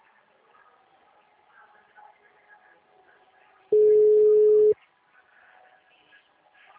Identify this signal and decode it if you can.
Ringback tone (Europe, 425 Hz).